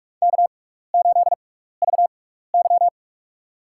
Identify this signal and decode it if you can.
Morse code, heard as a keyed tone, about 41 wpm, spaced as for 15 wpm, 700 Hz. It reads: K8VY
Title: USA Call Signs - 6x Character Spacing - 45wpm